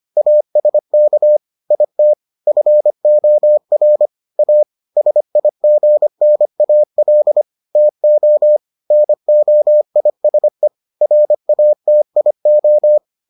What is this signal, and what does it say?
Morse code, heard as a keyed tone, about 25 wpm, 600 Hz. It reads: ASK IT FOR A SIGNAL TO NOISE RATIO